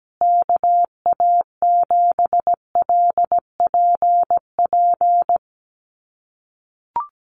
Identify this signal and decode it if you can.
Morse code, heard as a keyed tone, about 17 wpm, 700 Hz. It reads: KA7LPP
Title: USA Call Signs Repeated 2x 17wpm